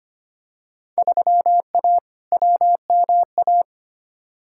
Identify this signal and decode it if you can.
Morse code, heard as a keyed tone, about 25 wpm, 700 Hz. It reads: 3A WMA
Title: Field Day Exchanges - 25wpm